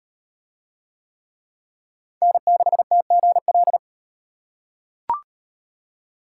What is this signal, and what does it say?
Morse code, heard as a keyed tone, about 35 wpm, 700 Hz. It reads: N6TGL